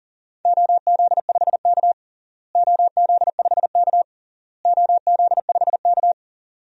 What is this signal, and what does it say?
Morse code, heard as a keyed tone, about 40 wpm, 700 Hz. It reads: OZ5K OZ5K OZ5K
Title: DXCC-100 Call Signs - 4 Characters - Repeated 3x - 40wpm